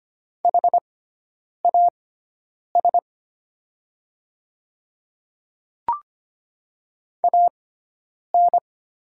Morse code, heard as a keyed tone, about 25 wpm, spaced as for 9 wpm, 700 Hz. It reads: HAS AN
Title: 6x Character Spacing - Top 200 Words - 25wpm